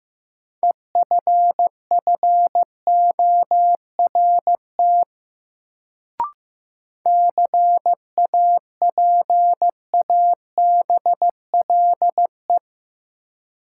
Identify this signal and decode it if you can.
Morse code, heard as a keyed tone, about 15 wpm, 700 Hz. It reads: EFFORT CAPABLE
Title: Sentences from Top 1200 Words - New Word Drill - 15wpm